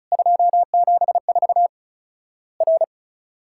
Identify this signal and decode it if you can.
Morse code, heard as a keyed tone, about 36 wpm, 700 Hz. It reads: 274 R
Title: Worked All Europe DX Contest - QTC Exchange - 35wpm